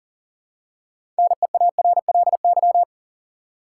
Morse code, heard as a keyed tone, about 40 wpm, 700 Hz. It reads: NEARLY
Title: Six Letter Words - 40wpm